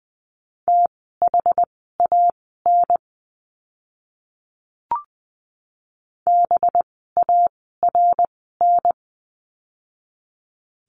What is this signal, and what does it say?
Morse code, heard as a keyed tone, about 20 wpm, spaced as for 15 wpm, 700 Hz. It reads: THAN BARN
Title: 185 Lesson 30f - Letter b - Words 20wpm